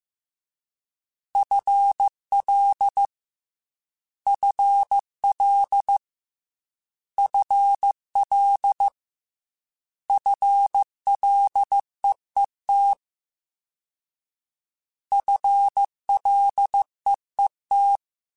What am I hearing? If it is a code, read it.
Morse code, heard as a keyed tone, about 15 wpm, 800 Hz. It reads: FL FL FL FLEET FLEET
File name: FL / FL mp3s / FL Prefixes 3x2 Twice @13